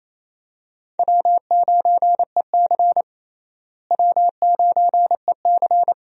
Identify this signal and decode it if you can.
Morse code, heard as a keyed tone, about 27 wpm, 700 Hz. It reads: W9EC W9EC
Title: USA Call Signs - 4 Characters - Repeated 2x - 28wpm